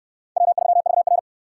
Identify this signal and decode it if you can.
Morse code, heard as a keyed tone, about 68 wpm, 700 Hz. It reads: W3VR